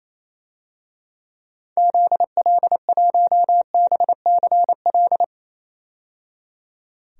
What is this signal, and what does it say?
Morse code, heard as a keyed tone, about 28 wpm, 700 Hz. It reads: ZL1BCL
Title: DXCC-100 Call Signs - 6 Characters - 28wpm